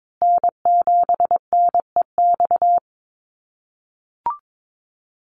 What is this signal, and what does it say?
Morse code, heard as a keyed tone, about 22 wpm, 700 Hz. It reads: N7NEX
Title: USA Call Signs Repeated 2x 22wpm